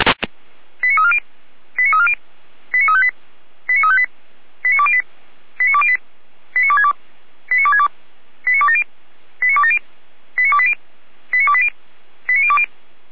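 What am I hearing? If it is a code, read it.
Five-tone selective call (ZVEI), heard as quick runs of five tones, tones of about 70 ms. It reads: 89340 89340 89358 89358 89208 89208 89262 89262 89270 89270 89280 89280 89020
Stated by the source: Five-tone ZVEI selective call 89303 received on an SDR.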